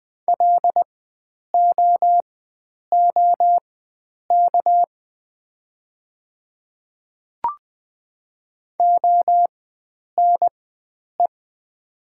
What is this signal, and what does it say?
Morse code, heard as a keyed tone, about 20 wpm, spaced as for 9 wpm, 700 Hz. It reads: LOOK ONE
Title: 4x Character Spacing - Top 100 Words - 20wpm